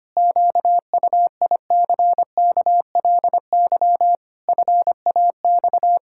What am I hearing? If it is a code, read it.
Morse code, heard as a keyed tone, about 25 wpm, 700 Hz. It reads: QUICKLY FAX
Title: Pangram Challenge v2 - 25wpm